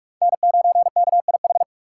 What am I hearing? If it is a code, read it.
Morse code, heard as a keyed tone, about 46 wpm, 700 Hz. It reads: N9KIH
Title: USA Call Signs - 5 Characters - 4x Character Spacing - 45wpm